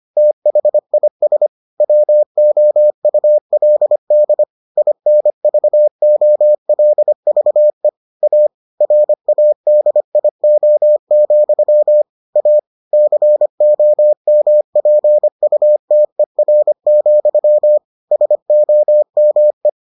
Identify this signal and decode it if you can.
Morse code, heard as a keyed tone, about 25 wpm, 600 Hz. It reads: THIS WOULD INVOLVE A RADIO, A COMPUTER, SOME